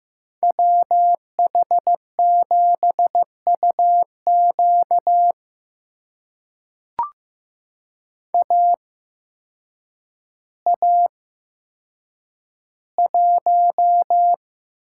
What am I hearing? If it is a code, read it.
Morse code, heard as a keyed tone, about 15 wpm, spaced as for 8 wpm, 700 Hz. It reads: WH7UQ AA1
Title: USA Call Signs - 5 Characters - 8x Character Spacing - 15wpm